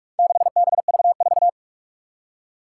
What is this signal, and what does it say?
Morse code, heard as a keyed tone, about 44 wpm, 700 Hz. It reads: 6BV4
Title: Four Challenging Letter-Number Combinations 45wpm